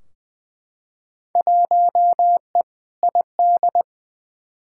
Morse code, heard as a keyed tone, about 20 wpm, 700 Hz. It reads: 1E ID
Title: Field Day Exchanges - 20wpm